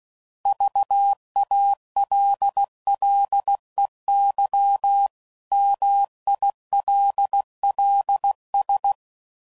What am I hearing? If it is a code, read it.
Morse code, heard as a keyed tone, about 16 wpm, 800 Hz. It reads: VALLEY MILLS
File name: CW Code Groups @15WPM